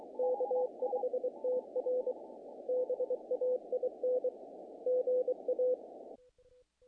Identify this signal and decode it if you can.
Morse code, heard as a keyed tone, about 24 wpm, 500 Hz. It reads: K5TR BAIN GA
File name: contests / naqp-cw-k5tr-op-k5pi- -1800